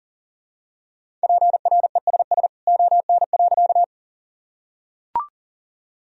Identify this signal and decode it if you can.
Morse code, heard as a keyed tone, about 40 wpm, 700 Hz. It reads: PRESS ON.
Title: Sentences from Top 1000 Words - Review Part 1 - 40wpm